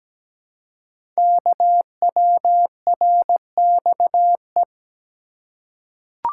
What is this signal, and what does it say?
Morse code, heard as a keyed tone, about 17 wpm, 700 Hz. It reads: KWRXE
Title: Code Groups - 5 Letters - 17wpm